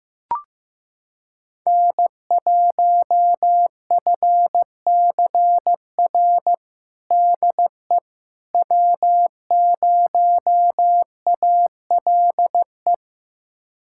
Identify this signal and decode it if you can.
Morse code, heard as a keyed tone, about 15 wpm, 700 Hz. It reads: N1FCR DE W0ALE
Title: USA Callsign DE Callsign - 5 Characters - 15wpm